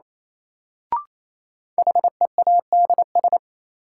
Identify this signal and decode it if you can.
Morse code, heard as a keyed tone, about 27 wpm, 700 Hz. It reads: HEADS
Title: Sentences from Top 1000 Words - Review Part 3 - New Word Drill - 28wpm